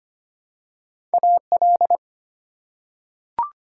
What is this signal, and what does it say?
Morse code, heard as a keyed tone, about 25 wpm, 700 Hz. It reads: AL
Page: morse-code-ninja-practice-25wpm